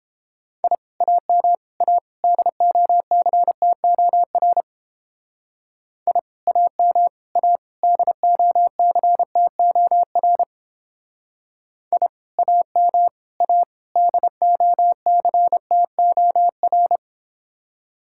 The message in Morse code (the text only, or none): I AM A DOCTOR I AM A DOCTOR I AM A DOCTOR